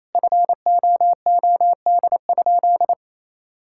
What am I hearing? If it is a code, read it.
Morse code, heard as a keyed tone, about 28 wpm, 700 Hz. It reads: FOOD?
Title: Sentences from Top 1000 Words - Review Part 5 - 28wpm